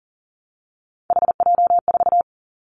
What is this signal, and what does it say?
Morse code, heard as a keyed tone, about 40 wpm, 700 Hz. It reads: HJ4
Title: Callsign Prefixes for Top 100 DXCC - 40wpm